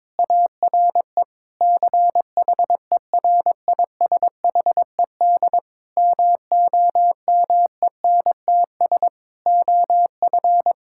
Morse code, heard as a keyed tone, about 22 wpm, 700 Hz. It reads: ARE CHERISHED MOMENTS OF
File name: Difficult Sentences with Hints - 22wpm